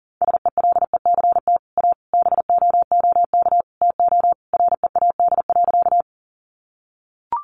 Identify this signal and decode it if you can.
Morse code, heard as a keyed tone, about 40 wpm, 700 Hz. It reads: SELECT A BOOK TO READ.
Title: Sentences from Top 1300 Words 40wpm